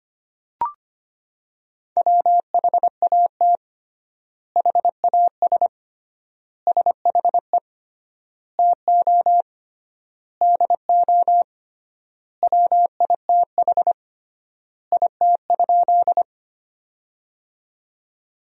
Morse code, heard as a keyed tone, about 25 wpm, 700 Hz. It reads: WHAT HAS SHE TO DO WITH IT?
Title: 3x Word-Spacing - Sentences from Top 100 Words - 25wpm